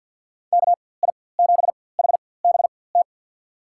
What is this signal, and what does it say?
Morse code, heard as a keyed tone, about 51 wpm, spaced as for 23 wpm, 700 Hz. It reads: KI7HBT
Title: USA Call Signs - 6 Characters - 4x Character Spacing - 50wpm